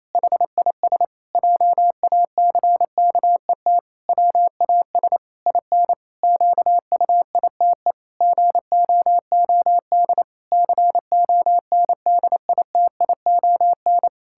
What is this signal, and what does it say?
Morse code, heard as a keyed tone, about 28 wpm, 700 Hz. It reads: HIS JACKET WAS IN QUITE GOOD CONDITION
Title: Pangram Challenge v2 - 28wpm